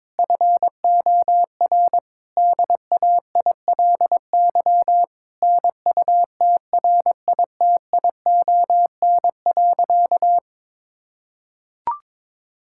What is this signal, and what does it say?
Morse code, heard as a keyed tone, about 22 wpm, 700 Hz. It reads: FOR DAILY NUTRITION.
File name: Difficult Sentences with Hints - 22wpm